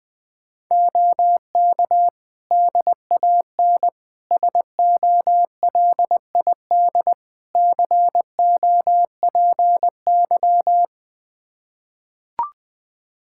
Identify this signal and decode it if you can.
Morse code, heard as a keyed tone, about 21 wpm, 700 Hz. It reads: OK DAN SOLID COPY
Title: Practice QSO - Exchange 3 - More Info - 20wpm